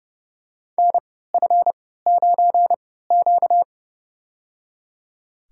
Morse code, heard as a keyed tone, about 30 wpm, spaced as for 17 wpm, 700 Hz. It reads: NF9Q
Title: USA Call Signs - 4 Characters - 3x Character Spacing - 30wpm